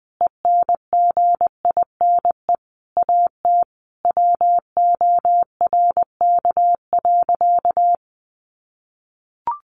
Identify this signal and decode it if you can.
Morse code, heard as a keyed tone, about 20 wpm, 700 Hz. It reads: ENGINE AT WORK.
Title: Sentences from Top 1500 Words 20wpm